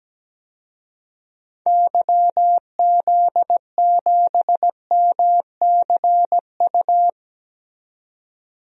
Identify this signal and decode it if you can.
Morse code, heard as a keyed tone, about 17 wpm, 700 Hz. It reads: YZ7MCU
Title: DXCC-100 Call Signs - 6 Characters - 17wpm